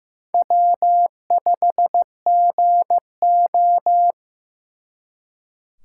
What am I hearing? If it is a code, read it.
Morse code, heard as a keyed tone, about 18 wpm, 700 Hz. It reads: W5GO
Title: Speed Racing - USA Call Signs 15wpm